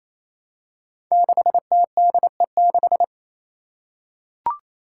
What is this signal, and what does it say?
Morse code, heard as a keyed tone, about 27 wpm, 700 Hz. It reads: 6TDE6
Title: Five Challenging Letter-Number Combinations 28wpm